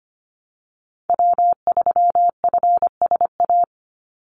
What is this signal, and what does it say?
Morse code, heard as a keyed tone, about 25 wpm, 700 Hz. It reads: W3FSA
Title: Call Signs Repeated 2x - 25wpm